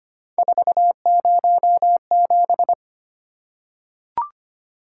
Morse code, heard as a keyed tone, about 25 wpm, 700 Hz. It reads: 407
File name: Three Letter-Number Combinations - 25wpm